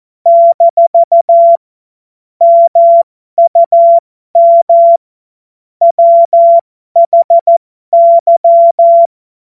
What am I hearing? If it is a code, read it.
Morse code, heard as a keyed tone, about 14 wpm, spaced as for 12 wpm, 680 Hz. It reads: - MUM WHY
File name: Morsekurs / Witze